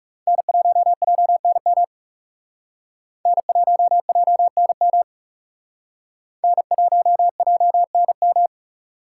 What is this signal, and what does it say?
Morse code, heard as a keyed tone, about 38 wpm, 700 Hz. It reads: N1JNM N1JNM N1JNM
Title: Speed Racing - USA Call Signs 30wpm